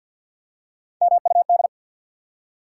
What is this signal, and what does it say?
Morse code, heard as a keyed tone, about 48 wpm, 700 Hz. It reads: MUD
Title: Three Letter Words - 50wpm